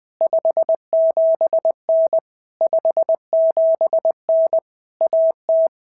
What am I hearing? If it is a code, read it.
Morse code, heard as a keyed tone, about 19 wpm, 650 Hz. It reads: 57N 57N AT
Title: POTA QSO Practice - 20wpm